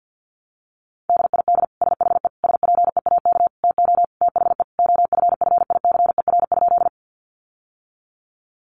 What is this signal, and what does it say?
Morse code, heard as a keyed tone, about 50 wpm, 700 Hz. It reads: DID SHE SPEAK TO THE OFFICER?
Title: Sentences from Top 1200 Words 50wpm